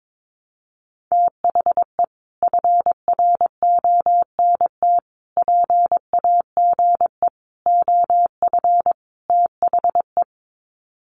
Morse code, heard as a keyed tone, about 22 wpm, 700 Hz. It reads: THE FRONT PAGE OF THE
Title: Sets of 5 Words Encore 22wpm